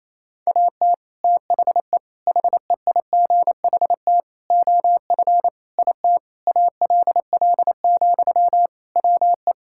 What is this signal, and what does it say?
Morse code, heard as a keyed tone, about 28 wpm, 700 Hz. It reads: AT THE HEIGHT OF IT ALL, WE
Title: Sentences from Top 1600 Words 28wpm